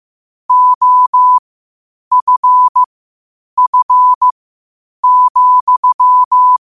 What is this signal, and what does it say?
Morse code, heard as a keyed tone, about 15 wpm, spaced as for 8 wpm, 1000 Hz. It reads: OFF,
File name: qotdfiles / 8wpm